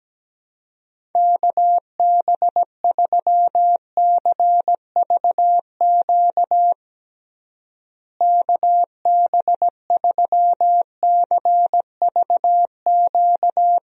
Morse code, heard as a keyed tone, about 17 wpm, 700 Hz. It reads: KB3CVQ KB3CVQ
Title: USA Call Signs - 6 Characters - Repeated 3x - 17wpm